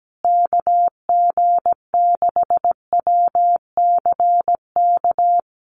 Morse code, heard as a keyed tone, about 17 wpm, 700 Hz. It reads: KG6WCK
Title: USA Call Signs Repeated 3x 17wpm